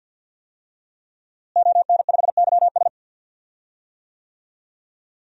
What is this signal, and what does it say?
Morse code, heard as a keyed tone, about 51 wpm, 700 Hz. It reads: ON5YS
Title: DXCC-100 Call Signs - 5 Characters - Repeated 3x - 50wpm